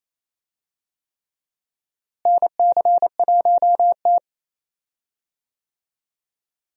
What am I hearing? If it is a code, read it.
Morse code, heard as a keyed tone, about 28 wpm, 700 Hz. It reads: NC1T
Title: USA Call Signs - 4 Characters - Repeated 3x - 28wpm